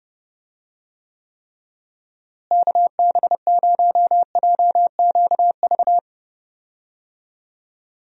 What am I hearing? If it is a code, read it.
Morse code, heard as a keyed tone, about 30 wpm, 700 Hz. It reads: KB0JQV
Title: USA Call Signs - 6 Characters - Repeated 3x - 30wpm